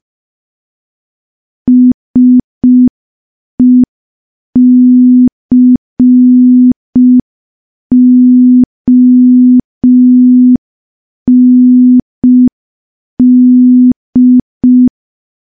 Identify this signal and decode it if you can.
Morse code, heard as a keyed tone, about 5 wpm, 250 Hz. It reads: SECOND
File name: morse / morse-training-05WPM